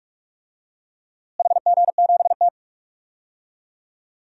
Morse code, heard as a keyed tone, about 46 wpm, 700 Hz. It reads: HG7T
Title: DXCC-100 Call Signs - 4 Characters - Repeated 2x - 45wpm